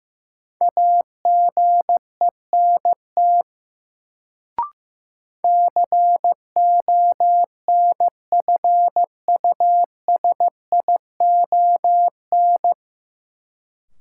Morse code, heard as a keyed tone, about 15 wpm, 700 Hz. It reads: AGENT CONFUSION
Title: Sentences from Top 1600 Words - New Word Drill Alt - 15wpm